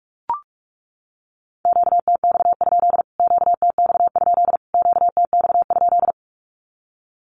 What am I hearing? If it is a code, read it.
Morse code, heard as a keyed tone, about 44 wpm, 700 Hz. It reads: QTX? QTX? QTX?
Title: Q-Codes Repeated 3x 45wpm